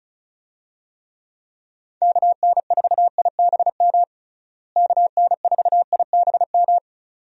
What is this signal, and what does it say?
Morse code, heard as a keyed tone, about 34 wpm, 700 Hz. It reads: KN4IBM KN4IBM
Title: USA Call Signs - 6 Characters - Repeated 2x - 35wpm